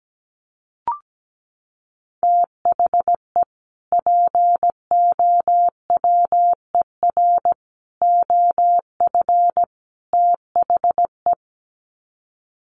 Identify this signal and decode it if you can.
Morse code, heard as a keyed tone, about 17 wpm, 700 Hz. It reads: THE POWER OF THE
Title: Sets of 4 Words 17wpm